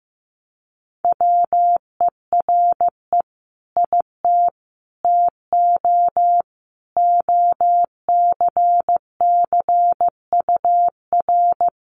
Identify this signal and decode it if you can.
Morse code, heard as a keyed tone, about 15 wpm, 700 Hz. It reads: WERE IT TO OCCUR
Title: Sentences from Top 1300 Words 15wpm